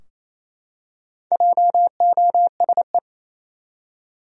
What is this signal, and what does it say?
Morse code, heard as a keyed tone, about 28 wpm, 700 Hz. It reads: JOSE
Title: USA Common Names - Max 5 Letters - 28wpm